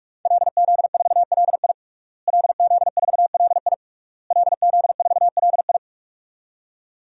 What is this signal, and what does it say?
Morse code, heard as a keyed tone, about 41 wpm, 700 Hz. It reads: LZ4LI LZ4LI LZ4LI
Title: DXCC-100 Call Signs - 5 Characters - Repeated 3x - 45wpm